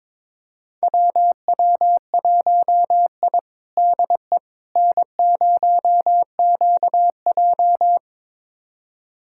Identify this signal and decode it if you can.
Morse code, heard as a keyed tone, about 21 wpm, 700 Hz. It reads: WW1I DE N0QJ
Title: USA Callsign DE Callsign - 4 Characters - 22wpm